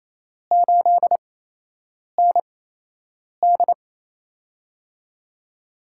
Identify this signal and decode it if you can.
Morse code, heard as a keyed tone, about 28 wpm, spaced as for 8 wpm, 700 Hz. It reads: 8ND
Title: USA Call Signs - 4 Characters - 8x Character Spacing - 28wpm